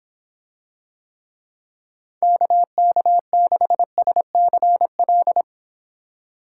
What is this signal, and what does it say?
Morse code, heard as a keyed tone, about 23 wpm, 700 Hz. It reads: KK6SCL